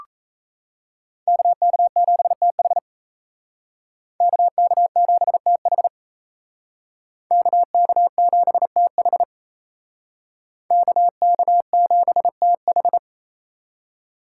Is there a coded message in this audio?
Yes: Morse code from 1.3 to 13.0 s, heard as a keyed tone, about 37 wpm, 700 Hz. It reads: KK7TH KK7TH KK7TH KK7TH